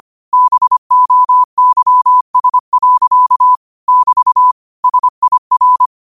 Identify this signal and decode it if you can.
Morse code, heard as a keyed tone, about 25 wpm, 1000 Hz. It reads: BOYS. <BT> SIR
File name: 25wpm